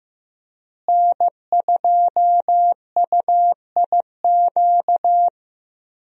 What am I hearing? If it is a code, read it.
Morse code, heard as a keyed tone, about 15 wpm, 700 Hz. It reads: N2UIQ
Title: USA Call Signs - 5 Characters - Repeated 2x - 15wpm